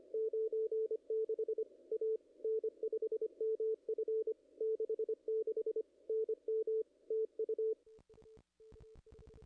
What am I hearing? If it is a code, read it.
Morse code, heard as a keyed tone, about 25 wpm, 410 Hz. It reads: 96 A N5MF 66 NM TU
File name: contests / ss-cw-k5tr-op-n5rz -0100